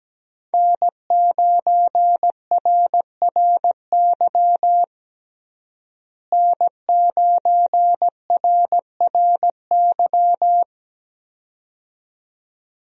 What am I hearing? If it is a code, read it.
Morse code, heard as a keyed tone, about 17 wpm, 700 Hz. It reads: N9RRY N9RRY
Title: USA Call Signs - 5 Characters - Repeated 2x - 17wpm